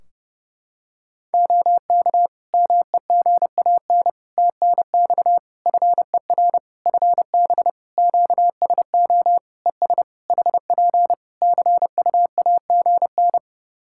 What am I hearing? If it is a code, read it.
Morse code, heard as a keyed tone, about 30 wpm, 700 Hz. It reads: OK MEGAN TNX FER FB QSO ES HP CUAGN
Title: Practice QSO - Exchange 4 - Ending - 30wpm